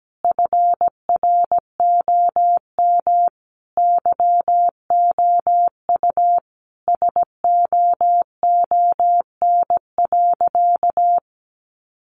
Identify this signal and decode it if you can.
Morse code, heard as a keyed tone, about 17 wpm, 700 Hz. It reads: FROM YOU SOON.
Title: Sentences from Top 400 Words 17wpm